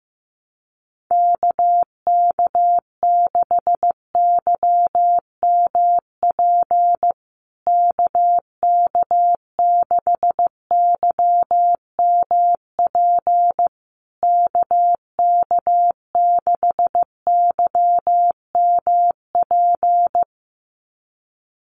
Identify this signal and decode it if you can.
Morse code, heard as a keyed tone, about 15 wpm, 700 Hz. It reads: KK6YMP KK6YMP KK6YMP
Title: USA Call Signs Repeated 3x 15wpm